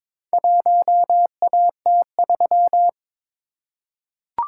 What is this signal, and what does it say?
Morse code, heard as a keyed tone, about 22 wpm, 700 Hz. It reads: 1AT3
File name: practice / 22wpm / Four Letter-Number Combinations - 22wpm